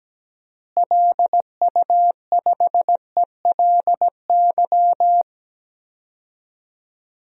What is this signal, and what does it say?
Morse code, heard as a keyed tone, about 17 wpm, 700 Hz. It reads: LU5ELY
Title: DXCC-100 Call Signs - 6 Characters - Repeated 2x - 17wpm